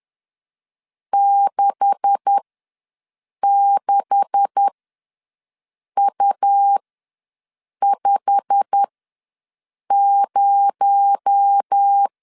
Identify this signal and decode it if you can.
Morse code, heard as a keyed tone, about 10 wpm, spaced as for 6 wpm, 800 Hz. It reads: 66U50
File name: LEZIONI / 9 286 ws 10wpm 4032